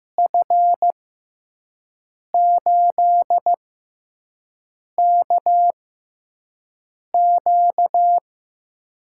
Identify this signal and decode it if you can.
Morse code, heard as a keyed tone, about 15 wpm, spaced as for 5 wpm, 700 Hz. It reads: F8KQ
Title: USA Call Signs - 5 Characters - 6x Character Spacing - 15wpm